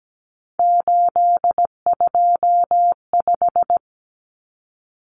Morse code, heard as a keyed tone, about 17 wpm, 700 Hz. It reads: 825
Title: Common Cut Numbers - 3-Digit 17wpm